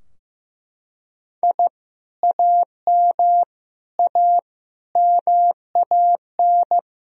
Morse code, heard as a keyed tone, about 15 wpm, 700 Hz. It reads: I AM A MAN
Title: Sentences from Top 1000 Words - Review Part 2 - 15wpm